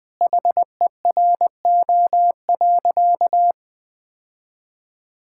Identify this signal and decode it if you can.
Morse code, heard as a keyed tone, about 20 wpm, 700 Hz. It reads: HERO.
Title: Sentences with 4-Word Phrases - 20wpm